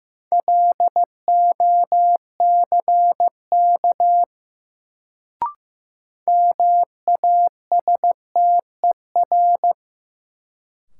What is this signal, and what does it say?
Morse code, heard as a keyed tone, about 15 wpm, 700 Hz. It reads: LOCK MASTER
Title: Sentences from Top 1300 Words - New Word Drill - 15wpm